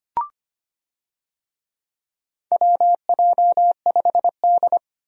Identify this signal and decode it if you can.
Morse code, heard as a keyed tone, about 25 wpm, 700 Hz. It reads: WJ5D
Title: USA Call Signs - 4 Characters - 25wpm